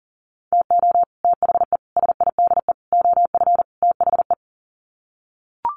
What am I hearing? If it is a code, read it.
Morse code, heard as a keyed tone, about 40 wpm, 700 Hz. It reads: TO THE SIDE OF THE
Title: Sets of 5 Words 40wpm